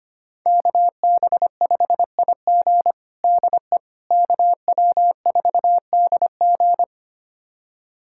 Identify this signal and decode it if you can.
Morse code, heard as a keyed tone, about 25 wpm, 700 Hz. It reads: KB5IG DE KW4DG
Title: USA Callsign DE Callsign - 5 Characters - 25wpm